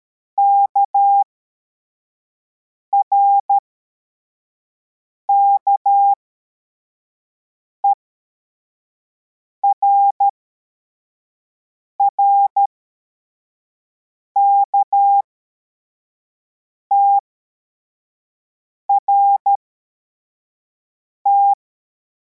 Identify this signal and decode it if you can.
Morse code, heard as a keyed tone, about 13 wpm, spaced as for 4 wpm, 800 Hz. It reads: KRKER RKTRT